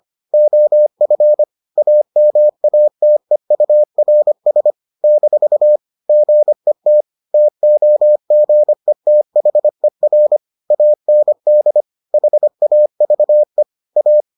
Morse code, heard as a keyed tone, about 25 wpm, 600 Hz. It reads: OF AMATEURS - GET TOGETHER AND HAVE A